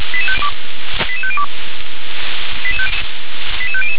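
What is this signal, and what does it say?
Five-tone selective call (ZVEI), heard as quick runs of five tones, tones of about 70 ms. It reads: 88502 88500
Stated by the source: ZVEI five-tone selective call received on SDR.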